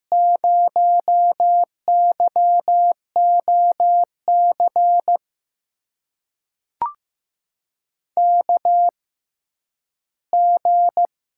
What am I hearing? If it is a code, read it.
Morse code, heard as a keyed tone, about 15 wpm, spaced as for 10 wpm, 700 Hz. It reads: 0YOC KG
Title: USA Call Signs - 5 Characters - 6x Character Spacing - 15wpm